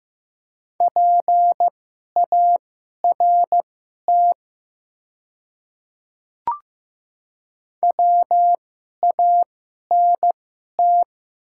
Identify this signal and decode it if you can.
Morse code, heard as a keyed tone, about 15 wpm, spaced as for 11 wpm, 700 Hz. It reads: PART WANT
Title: 2x Character Spacing - Top 200 Words - 15wpm